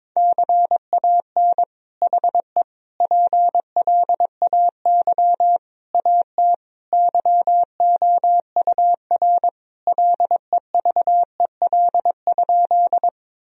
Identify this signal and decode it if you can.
Morse code, heard as a keyed tone, about 22 wpm, 700 Hz. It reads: CAN HE PLAY AT YOUR LEVEL?
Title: Sentences from Top 1000 Words - Review Part 4 - 22wpm